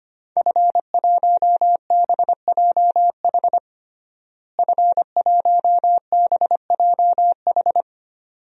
Morse code, heard as a keyed tone, about 25 wpm, 700 Hz. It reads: F1BJH F1BJH
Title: DXCC-100 Call Signs - 5 Characters - Repeated 2x - 25wpm